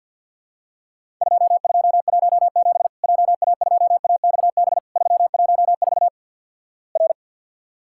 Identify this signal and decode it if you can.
Morse code, heard as a keyed tone, about 52 wpm, 700 Hz. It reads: 2217 JA2AXB 314 R
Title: Worked All Europe DX Contest - QTC Exchange - 50wpm